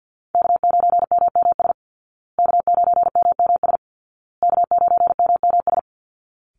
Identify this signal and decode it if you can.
Morse code, heard as a keyed tone, about 50 wpm, 700 Hz. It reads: K9MMS K9MMS K9MMS
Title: Call Signs Repeated 2x - 50wpm